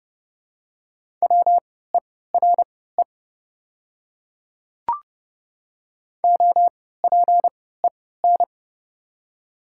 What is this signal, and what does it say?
Morse code, heard as a keyed tone, about 30 wpm, spaced as for 17 wpm, 700 Hz. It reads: WERE OPEN
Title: 3x Character Spacing - Top 500 Words - 30wpm